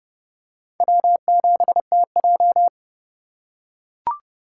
Morse code, heard as a keyed tone, about 30 wpm, 700 Hz. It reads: W7TJ
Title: USA Call Signs - 4 Characters - 2x Character Spacing - 30wpm